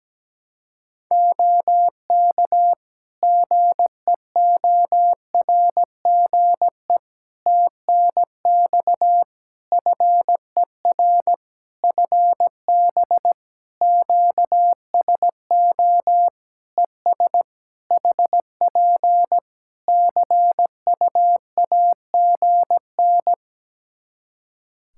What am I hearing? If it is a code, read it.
Morse code, heard as a keyed tone, about 17 wpm, 700 Hz. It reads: OK GEORGE TNX FER FB QSO ES HP CUAGN
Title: K1USN SST QSO Practice 17wpm